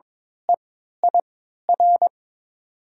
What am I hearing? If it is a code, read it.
Morse code, heard as a keyed tone, about 22 wpm, spaced as for 12 wpm, 700 Hz. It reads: EIR
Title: 3x Character Spacing - Top 500 Words - 22wpm